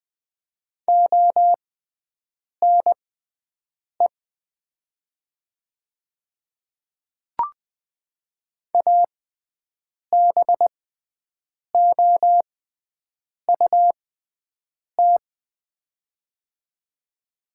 Morse code, heard as a keyed tone, about 20 wpm, spaced as for 7 wpm, 700 Hz. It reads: ONE ABOUT
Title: 6x Character Spacing - Top 500 Words - 20wpm